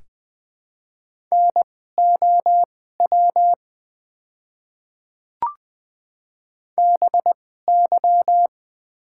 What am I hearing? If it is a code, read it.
Morse code, heard as a keyed tone, about 20 wpm, spaced as for 14 wpm, 700 Hz. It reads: NOW BY
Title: 2x Character Spacing - Top 100 Words - 20wpm